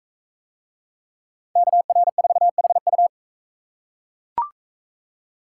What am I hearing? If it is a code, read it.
Morse code, heard as a keyed tone, about 42 wpm, 700 Hz. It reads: KR4HU